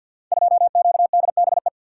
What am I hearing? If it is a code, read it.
Morse code, heard as a keyed tone, about 52 wpm, 700 Hz. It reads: 2QDBE
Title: Code Groups - 5 Letter:Numbers - 50wpm